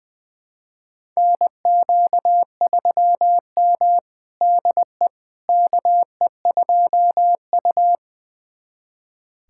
Morse code, heard as a keyed tone, about 21 wpm, 700 Hz. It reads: NQ3M DE KE2U
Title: USA Callsign DE Callsign - 4 Characters - 20wpm